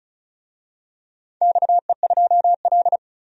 Morse code, heard as a keyed tone, about 34 wpm, 700 Hz. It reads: XE2L
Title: DXCC-100 Call Signs - 4 Characters - 35wpm